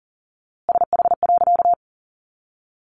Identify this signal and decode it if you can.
Morse code, heard as a keyed tone, about 40 wpm, 700 Hz. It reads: SH.
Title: A Sound Of Thunder 40wpm